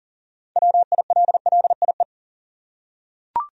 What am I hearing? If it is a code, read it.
Morse code, heard as a keyed tone, about 38 wpm, 700 Hz. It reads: WILLIE
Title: USA Common Names - 40wpm